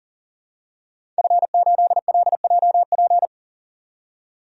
Morse code, heard as a keyed tone, about 38 wpm, 700 Hz. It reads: F8LJP
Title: DXCC-100 Call Signs - 5 Characters - 40wpm